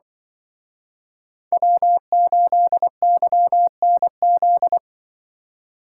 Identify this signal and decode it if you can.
Morse code, heard as a keyed tone, about 23 wpm, 700 Hz. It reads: W8YNZ
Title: Speed Racing - USA Call Signs 20wpm